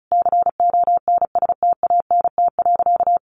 Morse code, heard as a keyed tone, about 34 wpm, 700 Hz. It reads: CONSTANT.
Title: Sentences from Top 1500 Words 35wpm